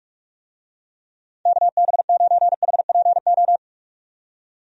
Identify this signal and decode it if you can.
Morse code, heard as a keyed tone, about 46 wpm, 700 Hz. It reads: KB9HPO
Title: USA Call Signs - 6 Characters - 45wpm